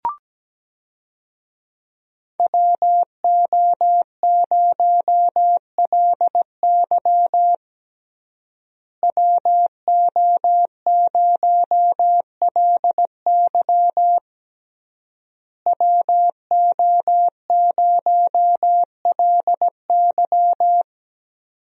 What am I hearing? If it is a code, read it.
Morse code, heard as a keyed tone, about 17 wpm, 700 Hz. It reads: WO0LY WO0LY WO0LY
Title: USA Call Signs - 5 Characters - Repeated 3x - 17wpm